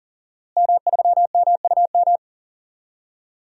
Morse code, heard as a keyed tone, about 40 wpm, 700 Hz. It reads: M3MUM
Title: DXCC-100 Call Signs - 5 Characters - Repeated 2x - 40wpm